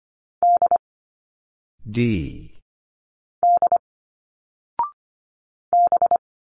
Morse code, heard as a keyed tone, about 25 wpm, 700 Hz. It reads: D D B
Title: Single Letters 25wpm